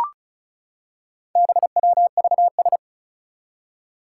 Morse code, heard as a keyed tone, about 36 wpm, 700 Hz. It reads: BWVS
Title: Four Challenging Letter-Number Combinations 35wpm